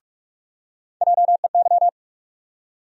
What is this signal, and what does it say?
Morse code, heard as a keyed tone, about 46 wpm, 700 Hz. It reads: JEY
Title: USA Call Sign Postfixes - 3 Letters 45wpm